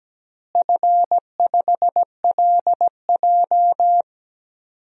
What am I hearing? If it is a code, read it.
Morse code, heard as a keyed tone, about 17 wpm, 700 Hz. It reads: F5LJ
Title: DXCC-100 Call Signs - 4 Characters - Repeated 2x - 17wpm